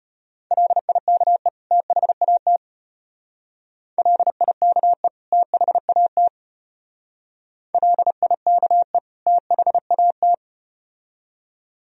Morse code, heard as a keyed tone, about 34 wpm, 700 Hz. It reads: LIKE THAT LIKE THAT LIKE THAT